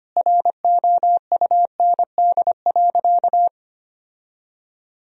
Morse code, heard as a keyed tone, about 25 wpm, 700 Hz. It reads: ROUND.
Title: Sentences from Top 1000 Words - Review Part 3 - 25wpm